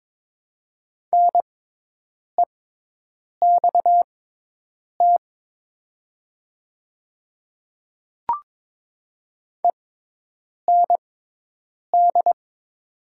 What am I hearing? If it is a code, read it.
Morse code, heard as a keyed tone, about 22 wpm, spaced as for 8 wpm, 700 Hz. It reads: NEXT END
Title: 6x Character Spacing - Top 500 Words - 22wpm